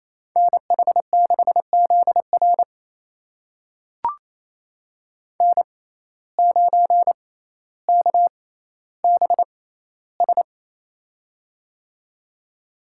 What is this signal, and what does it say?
Morse code, heard as a keyed tone, about 28 wpm, spaced as for 14 wpm, 700 Hz. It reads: NH6ZR N9KBS
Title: USA Call Signs - 5 Characters - 6x Character Spacing - 28wpm